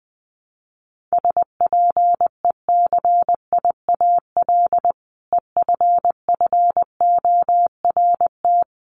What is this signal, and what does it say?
Morse code, heard as a keyed tone, about 20 wpm, 700 Hz. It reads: SPECIAL EFFORT
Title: Sentences from Top 1200 Words 20wpm